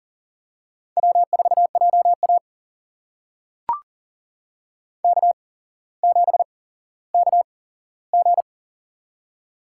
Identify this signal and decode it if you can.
Morse code, heard as a keyed tone, about 42 wpm, spaced as for 17 wpm, 700 Hz. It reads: W4JA K7KG
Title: USA Call Signs - 4 Characters - 8x Character Spacing - 40wpm